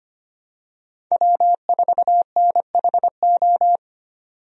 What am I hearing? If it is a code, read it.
Morse code, heard as a keyed tone, about 25 wpm, 700 Hz. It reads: W4NHO
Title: USA Call Signs - 5 Characters - 25wpm